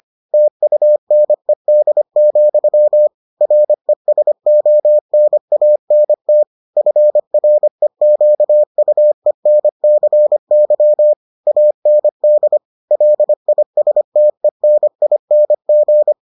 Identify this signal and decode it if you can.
Morse code, heard as a keyed tone, about 25 wpm, 600 Hz. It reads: TUNED, RESONANT FREQUENCY AND LISTENING